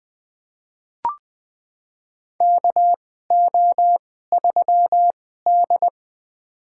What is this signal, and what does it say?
Morse code, heard as a keyed tone, about 20 wpm, spaced as for 15 wpm, 700 Hz. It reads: KO3D
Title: USA Call Signs - 4 Characters - 2x Character Spacing - 20wpm